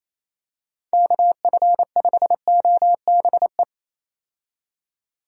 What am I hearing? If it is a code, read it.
Morse code, heard as a keyed tone, about 28 wpm, 700 Hz. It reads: KF5OBE
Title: USA Call Signs - 6 Characters - 28wpm